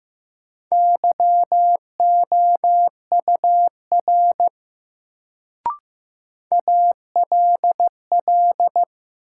Morse code, heard as a keyed tone, about 15 wpm, 700 Hz. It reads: YOUR ALL
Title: Sentences from Top 100 Words - New Word Drill - 15wpm